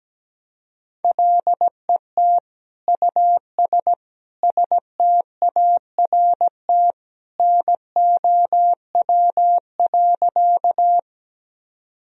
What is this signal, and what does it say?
Morse code, heard as a keyed tone, about 17 wpm, 700 Hz. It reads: LET US START NOW.
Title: Sentences from Top 1000 Words - Review Part 1 - 17wpm